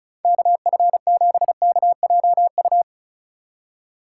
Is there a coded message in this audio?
Yes: Morse code from 0.2 to 2.8 s, heard as a keyed tone, about 44 wpm, 700 Hz. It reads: KF7KJU